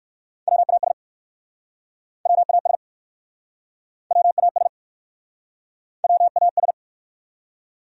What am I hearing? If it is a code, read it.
Morse code, heard as a keyed tone, about 59 wpm, 700 Hz. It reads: WAS WAS WAS WAS